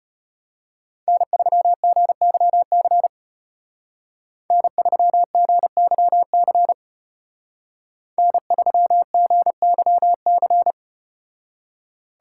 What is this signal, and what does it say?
Morse code, heard as a keyed tone, about 32 wpm, 700 Hz. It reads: N3GYC N3GYC N3GYC